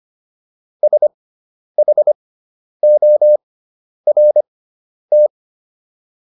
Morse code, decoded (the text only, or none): SHORT